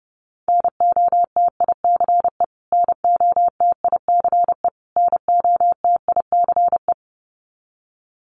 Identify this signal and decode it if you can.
Morse code, heard as a keyed tone, about 30 wpm, 700 Hz. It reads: NOTICE NOTICE NOTICE
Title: Top 500 Words Repeated 30wpm